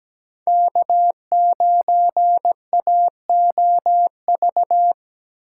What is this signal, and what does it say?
Morse code, heard as a keyed tone, about 17 wpm, 700 Hz. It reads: K9AOV
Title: USA Call Signs - 5 Characters - 3x Character Spacing - 17wpm